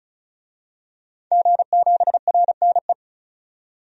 Morse code, heard as a keyed tone, about 36 wpm, 700 Hz. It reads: G7RNE
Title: DXCC-100 Call Signs - 5 Characters - 35wpm